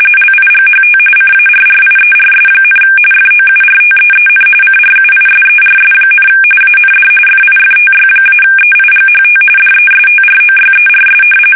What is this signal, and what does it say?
RTTY, about 45.45 baud. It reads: LIGHT CARRIERS, FOURTEEN CRUISERS, AND FIFTY-EIGHT DESTROYERS
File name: fsk850 / battleshipnj